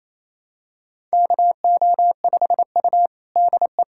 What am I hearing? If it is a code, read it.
Morse code, heard as a keyed tone, about 27 wpm, 700 Hz. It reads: KO5U DE
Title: USA Callsign DE Callsign - 4 Characters - 28wpm